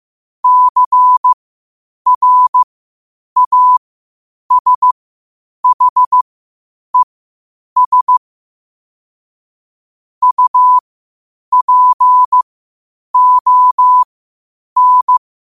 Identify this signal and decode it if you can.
Morse code, heard as a keyed tone, about 15 wpm, spaced as for 8 wpm, 1000 Hz. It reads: CRASHES UPON
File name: qotdfiles / 8wpm